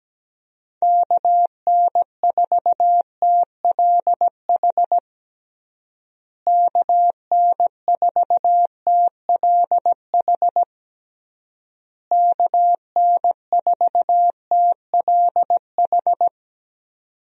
Morse code, heard as a keyed tone, about 17 wpm, 700 Hz. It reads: KN4TLH KN4TLH KN4TLH
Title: USA Call Signs - 6 Characters - Repeated 3x - 17wpm